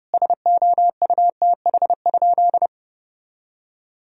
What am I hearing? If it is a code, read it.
Morse code, heard as a keyed tone, about 30 wpm, 700 Hz. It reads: SOUTH?
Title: Sentences from Top 1000 Words - Review Part 2 - 30wpm